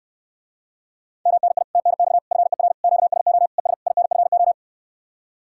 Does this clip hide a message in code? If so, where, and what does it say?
Yes: Morse code from 1.3 to 4.5 s, heard as a keyed tone, about 69 wpm, 700 Hz. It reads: MNI TNX FER QSO ES INFO